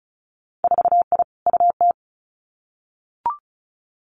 Morse code, heard as a keyed tone, about 36 wpm, 700 Hz. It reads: 4I UT
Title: Winter Field Day Exchanges 35wpm